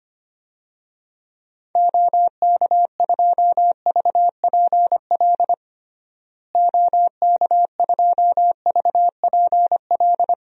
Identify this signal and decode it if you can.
Morse code, heard as a keyed tone, about 25 wpm, 700 Hz. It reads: OK2VPL OK2VPL
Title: DXCC-100 Call Signs - 6 Characters - Repeated 2x - 25wpm